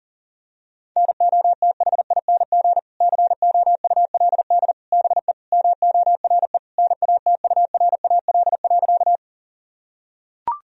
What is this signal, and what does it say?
Morse code, heard as a keyed tone, about 40 wpm, 700 Hz. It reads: NOTHING COULD BE MORE NATURAL.
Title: Sentences from Top 1000 Words - Review Part 4 - 40wpm